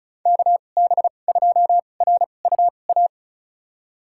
Morse code, heard as a keyed tone, about 34 wpm, spaced as for 25 wpm, 700 Hz. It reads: KB2RUA
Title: USA Call Signs - 2x Character Spacing - 35wpm